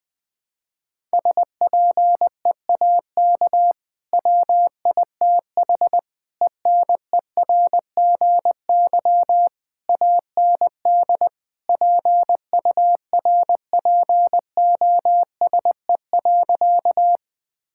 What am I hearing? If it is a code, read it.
Morse code, heard as a keyed tone, about 21 wpm, 700 Hz. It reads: SPEAK WITH ENERGY AND PURPOSE.
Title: Sentences from Top 1000 Words - Review Part 6 - 20wpm